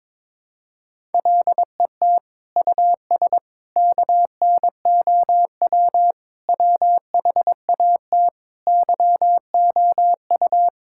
Morse code, heard as a keyed tone, about 22 wpm, 700 Hz. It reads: LET US KNOW WHAT YOU
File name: practice / 22wpm / Sets of 8 Words - 22wpm